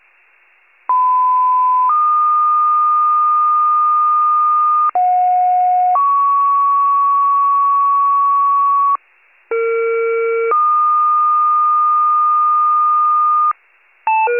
Two-tone page, heard as a one-second tone, then a three-second tone, about 1010 then 1220 Hz, about 730 then 1120 Hz, about 460 then 1220 Hz. It is heard from 0.9 to 13.5 s.
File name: wcalert